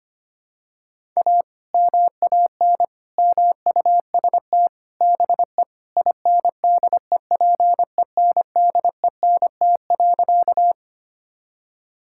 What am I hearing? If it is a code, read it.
Morse code, heard as a keyed tone, about 25 wpm, 700 Hz. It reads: A MAN MUST BE INDEPENDENT.
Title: Sentences from Top 1000 Words - Review Part 6 - 25wpm